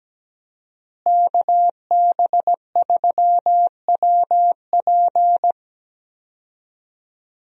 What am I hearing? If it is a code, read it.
Morse code, heard as a keyed tone, about 17 wpm, 700 Hz. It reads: KB3WP
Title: USA Call Signs - 5 Characters - Repeated 2x - 17wpm